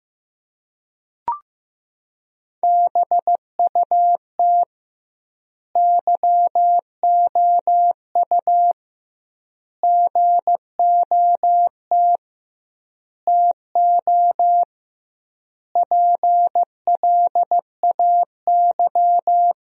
Morse code, heard as a keyed tone, about 16 wpm, 700 Hz. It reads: BUT YOU GOT TO PLAY
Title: 2x Word-Spacing - Sentences from Top 300 Words - 15wpm